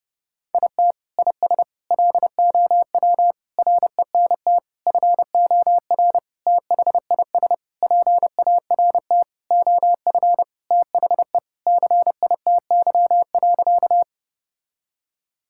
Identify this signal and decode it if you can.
Morse code, heard as a keyed tone, about 29 wpm, 700 Hz. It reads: IT IS LOW RENT FOR THIS PART OF THE CITY.
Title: Sentences from Top 1000 Words - Review Part 5 - 30wpm